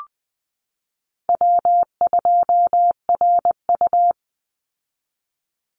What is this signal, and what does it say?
Morse code, heard as a keyed tone, about 20 wpm, 700 Hz. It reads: W2RU
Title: Call Signs Repeated 3x - 20wpm